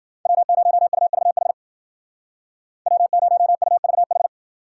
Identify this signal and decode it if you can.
Morse code, heard as a keyed tone, about 52 wpm, 700 Hz. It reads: W0UVH W0UVH